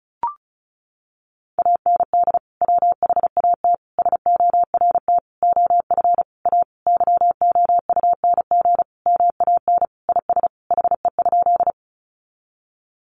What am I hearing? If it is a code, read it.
Morse code, heard as a keyed tone, about 34 wpm, 700 Hz. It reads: AND WHAT SORT OF A YOUNG MAN IS HE?
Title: Sentences from Top 1300 Words 35wpm